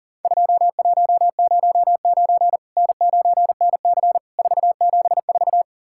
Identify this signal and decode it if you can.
Morse code, heard as a keyed tone, about 40 wpm, 700 Hz. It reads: 2109 N9NC 474
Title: Worked All Europe DX Contest - QTC Exchange - 40wpm